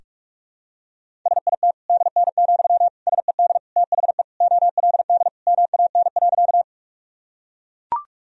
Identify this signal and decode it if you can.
Morse code, heard as a keyed tone, about 41 wpm, 690 Hz. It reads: SIT DN, SED THE OLD MAN.
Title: Sentences from Top 500 Words and Abbreviations 45wpm